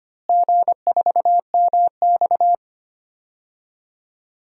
Morse code, heard as a keyed tone, about 25 wpm, 700 Hz. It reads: G4MX
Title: DXCC-100 Call Signs - 4 Characters - Repeated 3x - 25wpm